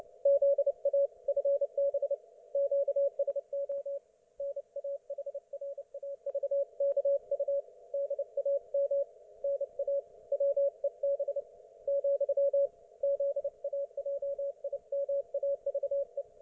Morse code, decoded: ZA FB QSO NAHRAVKU DAM NA WEB , ZAJIMAVE